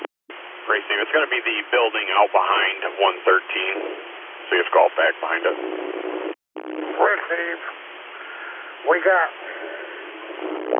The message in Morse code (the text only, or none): none